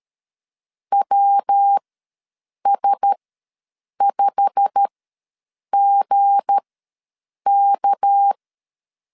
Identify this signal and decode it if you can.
Morse code, heard as a keyed tone, about 13 wpm, spaced as for 7 wpm, 800 Hz. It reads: WS5GK
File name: LEZIONI / 8 YJ45 / 8 YJ45 ws 12wpm 3146